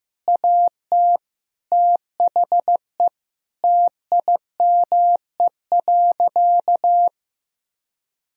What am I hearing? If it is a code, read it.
Morse code, heard as a keyed tone, about 15 wpm, 700 Hz. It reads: AT THE TIME.
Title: Sentences from Top 1000 Words - Review Part 4 - 15wpm